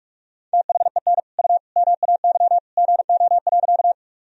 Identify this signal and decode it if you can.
Morse code, heard as a keyed tone, about 46 wpm, 700 Hz. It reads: THEN U MAY GO.
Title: Sentences from Top 100 Words and Abbreviations 45wpm